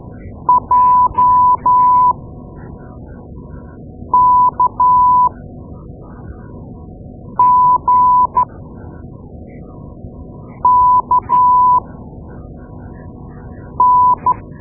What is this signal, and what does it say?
Morse code, heard as a keyed tone, about 12 wpm, spaced as for 4 wpm, 1000 Hz. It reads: JKGKN
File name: Morse Lessons / Morse Lessons / Tape-2A